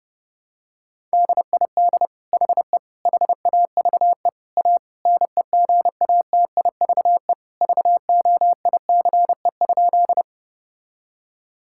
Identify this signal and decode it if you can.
Morse code, heard as a keyed tone, about 30 wpm, 700 Hz. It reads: DID HE HAVE A NEGATIVE VOICE?
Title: Sentences from Top 1000 Words - Review Part 6 - 30wpm